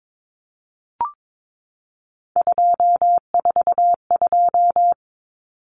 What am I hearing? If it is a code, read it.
Morse code, heard as a keyed tone, about 22 wpm, 700 Hz. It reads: 242
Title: Numbers - 3-Digit 22wpm